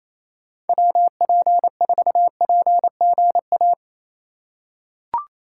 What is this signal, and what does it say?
Morse code, heard as a keyed tone, about 27 wpm, 700 Hz. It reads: WP4PGA
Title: USA Call Signs - 6 Characters - 6x Character Spacing - 28wpm